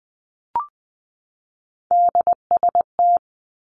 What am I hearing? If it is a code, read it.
Morse code, heard as a keyed tone, about 20 wpm, 700 Hz. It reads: DST
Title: Three Challenging Letter-Number Combinations 20wpm